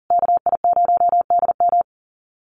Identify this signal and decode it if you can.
Morse code, heard as a keyed tone, about 40 wpm, 700 Hz. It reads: KI0DM
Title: USA Call Signs Repeated 3x 40wpm